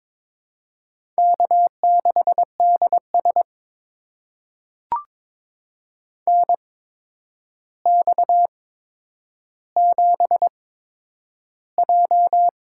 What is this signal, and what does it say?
Morse code, heard as a keyed tone, about 22 wpm, spaced as for 9 wpm, 700 Hz. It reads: K6DS NX7J
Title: USA Call Signs - 4 Characters - 8x Character Spacing - 22wpm